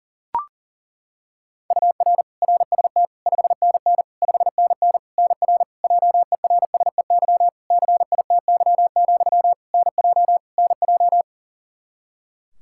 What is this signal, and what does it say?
Morse code, heard as a keyed tone, about 40 wpm, 700 Hz. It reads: UR RST 5NN 5NN NR JERSEY CITY, NJ NJ
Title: Practice QSO - Exchange 1 - Beginning - 40wpm